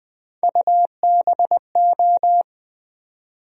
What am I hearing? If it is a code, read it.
Morse code, heard as a keyed tone, about 20 wpm, 700 Hz. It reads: UBO
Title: USA Call Sign Postfixes - 3 Letters 20wpm